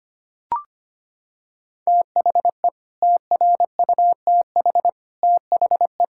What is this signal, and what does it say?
Morse code, heard as a keyed tone, about 25 wpm, 700 Hz. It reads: THE TRUTH THE
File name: Sets of 9 Words - 25wpm